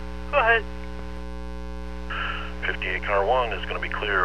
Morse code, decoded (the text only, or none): none